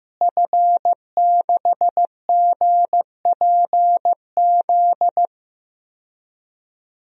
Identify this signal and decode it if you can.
Morse code, heard as a keyed tone, about 15 wpm, 700 Hz. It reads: F6GPZ
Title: DXCC-100 Call Signs - 5 Characters - Repeated 3x - 15wpm